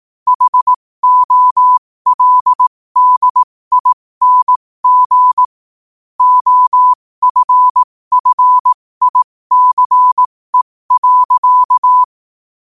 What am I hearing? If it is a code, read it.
Morse code, heard as a keyed tone, about 18 wpm, spaced as for 15 wpm, 1000 Hz. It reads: HOLDING OFFICE.
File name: qotdfiles / 15wpm